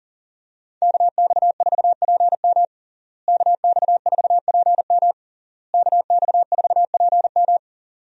Morse code, heard as a keyed tone, about 40 wpm, 700 Hz. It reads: KX4PM KX4PM KX4PM
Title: USA Call Signs - 5 Characters - Repeated 3x - 40wpm